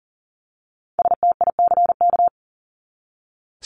Morse code, heard as a keyed tone, about 40 wpm, 700 Hz. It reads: STICK
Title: Top 1001-1250 Words 40wpm